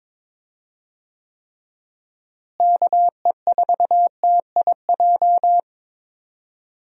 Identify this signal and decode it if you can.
Morse code, heard as a keyed tone, about 22 wpm, 700 Hz. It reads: KE4TIJ
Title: USA Call Signs - 6 Characters - Repeated 2x - 22wpm